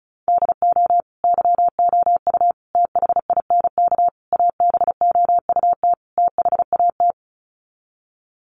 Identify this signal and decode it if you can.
Morse code, heard as a keyed tone, about 34 wpm, 700 Hz. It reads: DO YOU THINK ABOUT THAT
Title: Sets of 5 Words Encore2 35wpm